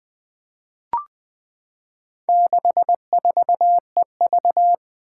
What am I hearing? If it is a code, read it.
Morse code, heard as a keyed tone, about 21 wpm, 700 Hz. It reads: 64EV
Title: Four Challenging Letter-Number Combinations 20wpm